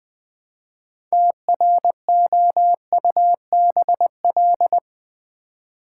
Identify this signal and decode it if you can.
Morse code, heard as a keyed tone, about 20 wpm, 700 Hz. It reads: TROUBL
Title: Build A Word - 3 Letters At A Time - 8-Letter Words - 20wpm